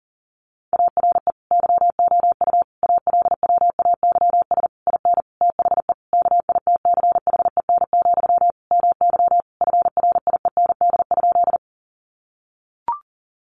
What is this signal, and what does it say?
Morse code, heard as a keyed tone, about 40 wpm, 700 Hz. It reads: ARE YOU ALWAYS IN THE KITCHEN, MY FRIEND?
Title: Sentences from Top 1400 Words 40wpm